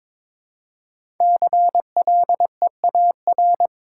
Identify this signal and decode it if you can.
Morse code, heard as a keyed tone, about 22 wpm, 700 Hz. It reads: CLEAR
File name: practice / 22wpm / 5 Letter Words - First 3 Letters Given - 22wpm